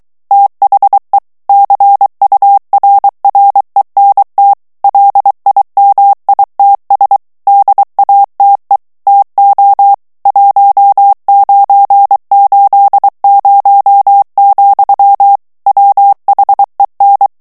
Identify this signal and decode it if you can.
Morse code, heard as a keyed tone, about 23 wpm, 800 Hz. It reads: THE CURRENT LIMITS DATE TO 1980, WHEN
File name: ANGLAIS texte clair 3 vitesse 22 mots minute